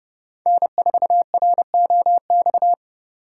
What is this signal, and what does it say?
Morse code, heard as a keyed tone, about 28 wpm, 700 Hz. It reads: N4ROX